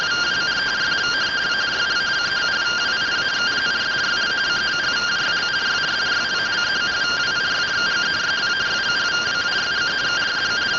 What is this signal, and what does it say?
Two-tone teleprinter signal, about 170 Hz shift.